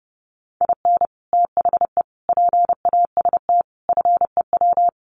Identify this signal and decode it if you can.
Morse code, heard as a keyed tone, about 30 wpm, 700 Hz. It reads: IN THE PAST FEW
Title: Sets of 5 Words 30wpm